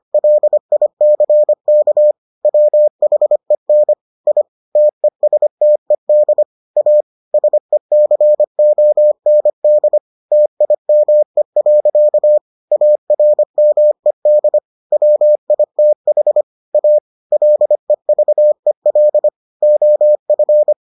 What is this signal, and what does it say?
Morse code, heard as a keyed tone, about 25 wpm, 600 Hz. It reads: LICK WHEN I TESTED A SECOND TIME. ARMED WITH A LEVEL OF